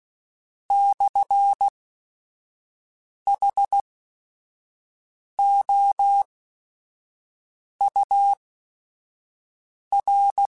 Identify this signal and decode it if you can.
Morse code, heard as a keyed tone, about 16 wpm, spaced as for 5 wpm, 800 Hz. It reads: /HOUR
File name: Technical Subjects / Technical Subjects-mp3s / Sw Pwr Sup @5